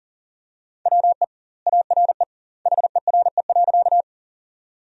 Morse code, heard as a keyed tone, about 40 wpm, 700 Hz. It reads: WE ARE HERE.
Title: 2x Word-Spacing - Sentences from Top 200 Words - 40wpm